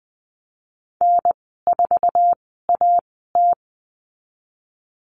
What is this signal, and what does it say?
Morse code, heard as a keyed tone, about 20 wpm, spaced as for 15 wpm, 700 Hz. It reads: N4AT
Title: 048 Lesson 9g - Number 4 - Call Signs 20wpm